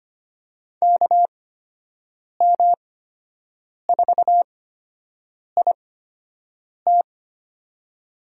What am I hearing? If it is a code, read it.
Morse code, heard as a keyed tone, about 25 wpm, spaced as for 7 wpm, 700 Hz. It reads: KM4IT
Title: USA Call Signs - 5 Characters - 8x Character Spacing - 25wpm